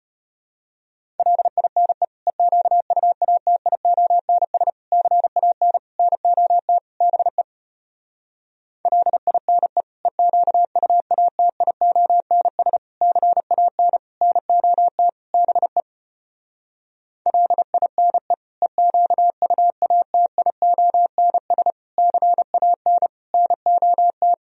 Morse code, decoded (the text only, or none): LINE EQUATIONS CAN NOT BE LINE EQUATIONS CAN NOT BE LINE EQUATIONS CAN NOT